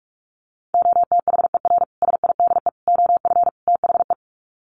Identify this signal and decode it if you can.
Morse code, heard as a keyed tone, about 44 wpm, 700 Hz. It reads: OTHER SIDE OF THE
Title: Sets of 4 Words 45wpm